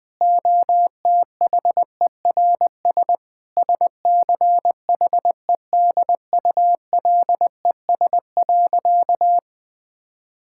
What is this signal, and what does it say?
Morse code, heard as a keyed tone, about 20 wpm, 700 Hz. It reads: OTHERS SCHEDULES.
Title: Difficult Sentences with Hints - 20wpm